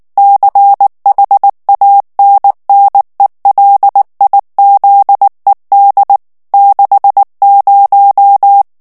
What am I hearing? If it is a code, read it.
Morse code, heard as a keyed tone, about 19 wpm, 800 Hz. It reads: CHANNELIZED 60
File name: ANGLAIS texte clair 3 vitesse 18 mots minute